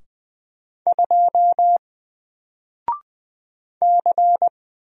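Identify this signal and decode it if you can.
Morse code, heard as a keyed tone, about 20 wpm, 700 Hz. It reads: 2 C
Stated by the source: Single letter and number Morse drill at 20 wpm, from N4LCD.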